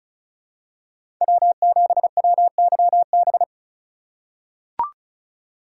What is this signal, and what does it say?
Morse code, heard as a keyed tone, about 34 wpm, 700 Hz. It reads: W7WYB
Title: USA Call Signs - 5 Characters - 2x Character Spacing - 35wpm